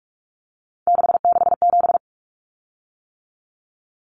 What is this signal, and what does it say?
Morse code, heard as a keyed tone, about 44 wpm, 700 Hz. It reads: 667
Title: Common Cut Numbers - 3-Digit 45wpm